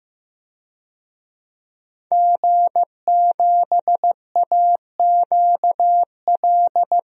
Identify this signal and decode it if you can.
Morse code, heard as a keyed tone, about 15 wpm, 700 Hz. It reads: G7AQL
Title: DXCC-100 Call Signs - 5 Characters - 15wpm